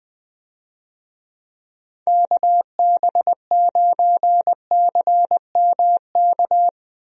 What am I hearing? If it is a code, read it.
Morse code, heard as a keyed tone, about 20 wpm, 700 Hz. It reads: KB9CMK
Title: USA Call Signs - 6 Characters - Repeated 2x - 20wpm